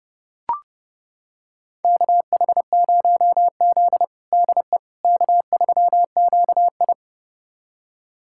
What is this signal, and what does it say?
Morse code, heard as a keyed tone, about 30 wpm, 700 Hz. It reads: KH0Z DE K3QI
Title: USA Callsign DE Callsign - 4 Characters - 30wpm